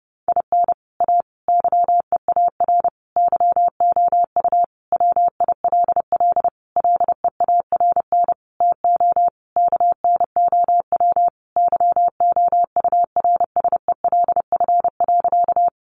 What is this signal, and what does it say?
Morse code, heard as a keyed tone, about 30 wpm, 700 Hz. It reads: IN A YEAR YOU WILL LEARN TO KNOW YOURSELF.
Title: Sentences from Top 600 Words 30wpm